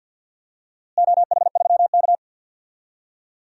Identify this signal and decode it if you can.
Morse code, heard as a keyed tone, about 51 wpm, 700 Hz. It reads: OH3K
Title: DXCC-100 Call Signs - 4 Characters - Repeated 3x - 50wpm